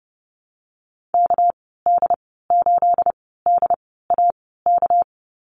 Morse code, heard as a keyed tone, about 30 wpm, spaced as for 17 wpm, 700 Hz. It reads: KD8DAK
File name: Lesson 37g - Letter x - Call Signs 30wpm